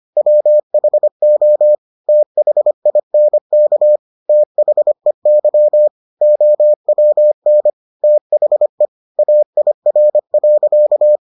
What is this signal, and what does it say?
Morse code, heard as a keyed tone, about 25 wpm, 600 Hz. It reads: WHO THINK THEY OWN THE AIR.